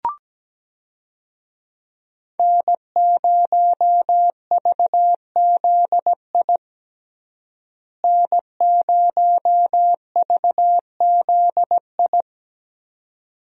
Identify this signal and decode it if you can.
Morse code, heard as a keyed tone, about 17 wpm, 700 Hz. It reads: N0VZI N0VZI
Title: USA Call Signs - 5 Characters - Repeated 2x - 17wpm